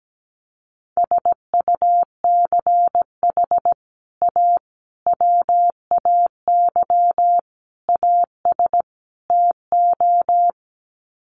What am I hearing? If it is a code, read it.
Morse code, heard as a keyed tone, about 17 wpm, 700 Hz. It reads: SUCH A WAY AS TO
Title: Sets of 5 Words Encore2 17wpm